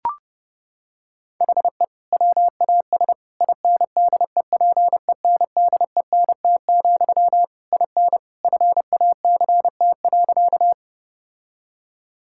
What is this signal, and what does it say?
Morse code, heard as a keyed tone, about 30 wpm, 700 Hz. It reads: HE WAS INDEPENDENT, IN FACT.
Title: Sentences from Top 1000 Words - Review Part 6 - 30wpm